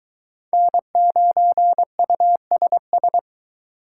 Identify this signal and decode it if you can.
Morse code, heard as a keyed tone, about 21 wpm, 700 Hz. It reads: N9USS